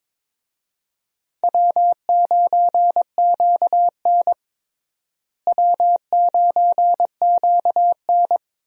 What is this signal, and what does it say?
Morse code, heard as a keyed tone, about 22 wpm, 700 Hz. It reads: W9QN W9QN
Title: USA Call Signs - 4 Characters - Repeated 2x - 22wpm